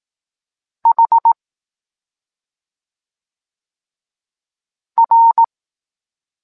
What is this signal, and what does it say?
Morse code, heard as a keyed tone, about 18 wpm, 920 Hz. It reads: H R
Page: Tekst 23 – Doskonalenie odbioru poznanych liter